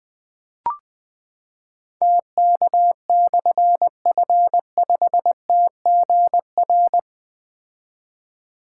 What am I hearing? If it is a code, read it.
Morse code, heard as a keyed tone, about 20 wpm, 700 Hz. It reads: TK/F5TGR
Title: Difficult Call Signs 20wpm